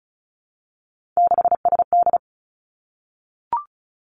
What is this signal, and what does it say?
Morse code, heard as a keyed tone, about 36 wpm, 700 Hz. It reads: 6SD
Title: Three Challenging Letter-Number Combinations 35wpm